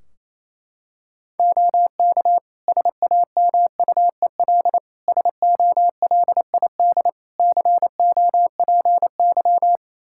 Morse code, heard as a keyed tone, about 28 wpm, 700 Hz. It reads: OK SAMUEL SOLID COPY
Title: K1USN SST QSO Practice 28wpm